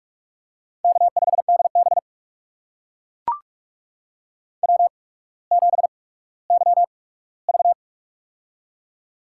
Morse code, heard as a keyed tone, about 44 wpm, spaced as for 19 wpm, 700 Hz. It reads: K5DB W7YV
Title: USA Call Signs - 8x Character Spacing - 45wpm